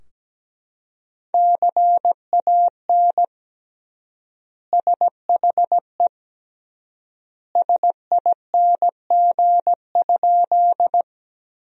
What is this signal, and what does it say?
Morse code, heard as a keyed tone, about 17 wpm, 700 Hz. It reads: CAN SHE SING?
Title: 3x Word-Spacing - Sentences from Top 400 Words - 17wpm